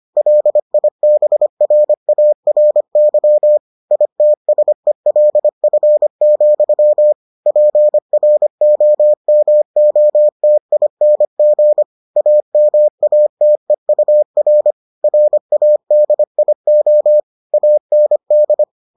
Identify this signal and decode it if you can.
Morse code, heard as a keyed tone, about 25 wpm, 600 Hz. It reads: LIBRARY ITSELF, PROMOTING AMATEUR RADIO AND